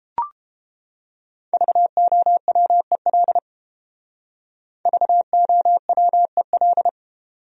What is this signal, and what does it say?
Morse code, heard as a keyed tone, about 29 wpm, 700 Hz. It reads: VOWEL VOWEL